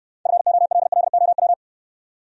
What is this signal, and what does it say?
Morse code, heard as a keyed tone, about 61 wpm, 700 Hz. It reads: FOLLOW